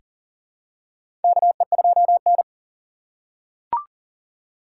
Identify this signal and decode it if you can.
Morse code, heard as a keyed tone, about 40 wpm, 700 Hz. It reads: KE2N
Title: USA Call Signs - 4 Characters - 3x Character Spacing - 40wpm